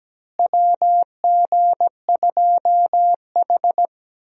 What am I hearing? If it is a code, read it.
Morse code, heard as a keyed tone, about 17 wpm, 700 Hz. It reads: WG2H
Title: USA Call Signs - 4 Characters - Repeated 2x - 17wpm